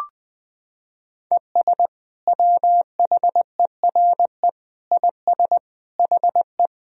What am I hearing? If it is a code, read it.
Morse code, heard as a keyed tone, about 20 wpm, 700 Hz. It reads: ES WHERE IS HE
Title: Sentences from Top 200 Words and Abbreviations 20wpm